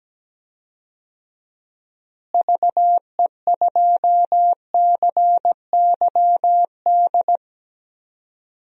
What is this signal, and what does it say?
Morse code, heard as a keyed tone, about 17 wpm, 700 Hz. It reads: VE2CYD
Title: DXCC-100 Call Signs - 6 Characters - Repeated 2x - 17wpm